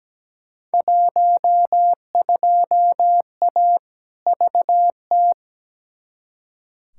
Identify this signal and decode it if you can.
Morse code, heard as a keyed tone, about 17 wpm, 700 Hz. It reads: 12A VT
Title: Field Day Exchanges Encore - 17wpm